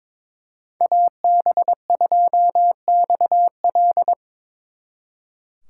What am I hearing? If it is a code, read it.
Morse code, heard as a keyed tone, about 26 wpm, 700 Hz. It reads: AB2XL
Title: Speed Racing - USA 2x2 Call Signs 22wpm